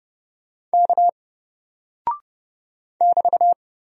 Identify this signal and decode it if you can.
Morse code, heard as a keyed tone, about 30 wpm, 700 Hz. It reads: K <BT>
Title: Single Character (Letter:Number:Punctuation:Prosign) Alt 30wpm